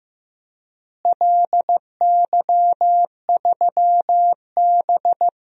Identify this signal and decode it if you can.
Morse code, heard as a keyed tone, about 16 wpm, 700 Hz. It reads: LY3B
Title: DXCC-100 Call Signs - 4 Characters - Repeated 3x - 15wpm